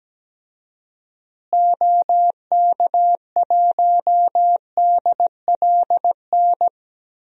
Morse code, heard as a keyed tone, about 17 wpm, 700 Hz. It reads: OK1DLN
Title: DXCC-100 Call Signs - 6 Characters - Repeated 3x - 17wpm